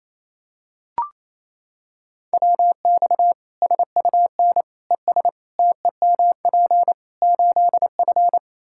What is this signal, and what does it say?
Morse code, heard as a keyed tone, about 28 wpm, 700 Hz. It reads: WX SUN ES TEMP 8F
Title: Practice QSO - Exchange 2 - Info - 28wpm